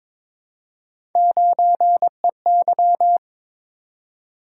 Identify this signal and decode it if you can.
Morse code, heard as a keyed tone, about 22 wpm, 700 Hz. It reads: 9EY
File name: practice / 22wpm / Three Letter-Number Combinations - 22wpm